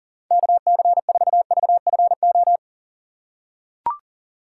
Morse code, heard as a keyed tone, about 40 wpm, 700 Hz. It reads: KC4VFO
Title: USA Call Signs - 2x Character Spacing - 40wpm